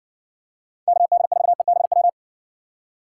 Morse code, heard as a keyed tone, about 57 wpm, 700 Hz. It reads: KD4EBW